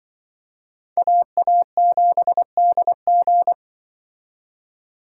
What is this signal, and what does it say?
Morse code, heard as a keyed tone, about 24 wpm, 700 Hz. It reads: AA7DG
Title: Speed Racing - USA 2x2 Call Signs 20wpm